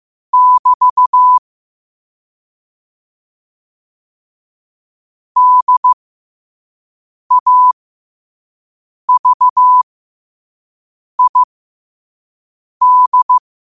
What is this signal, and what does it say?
Morse code, heard as a keyed tone, about 15 wpm, spaced as for 5 wpm, 1000 Hz. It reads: <BT> DAVID
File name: qotdfiles / 5wpm